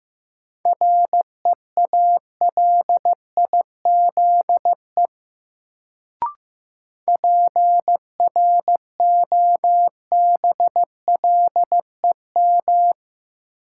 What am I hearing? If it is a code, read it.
Morse code, heard as a keyed tone, about 15 wpm, 700 Hz. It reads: REALIZE PROBLEM
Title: Sentences from Top 1000 Words - New Word Drill - 15wpm